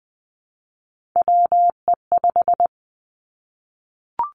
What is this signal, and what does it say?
Morse code, heard as a keyed tone, about 20 wpm, 700 Hz. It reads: WE5
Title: Three Challenging Letter-Number Combinations 20wpm